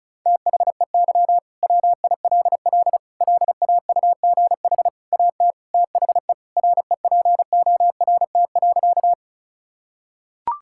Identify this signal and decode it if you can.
Morse code, heard as a keyed tone, about 36 wpm, 700 Hz. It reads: THEY WILL LAUGH AT THE REPORT.
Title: Sentences from Top 1000 Words - Review Part 6 - 35wpm